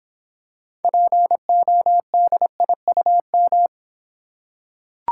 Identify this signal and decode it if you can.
Morse code, heard as a keyed tone, about 26 wpm, 700 Hz. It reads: PODIUM